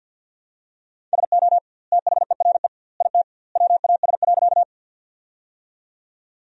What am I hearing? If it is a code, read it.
Morse code, heard as a keyed tone, about 50 wpm, 700 Hz. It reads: SO THERE IT WAS.
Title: 2x Word-Spacing - Sentences from Top 100 Words - 50wpm